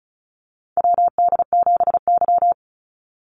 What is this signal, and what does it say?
Morse code, heard as a keyed tone, about 34 wpm, 700 Hz. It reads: WD7Y
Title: NA SOTA Chaser Callsigns 35wpm